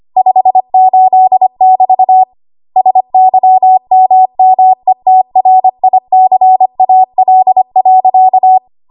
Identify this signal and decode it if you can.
Morse code, heard as a keyed tone, about 25 wpm, 750 Hz. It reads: 58<BT> SYMMETRICAL.
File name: Archive / 25 WPM / 25WPM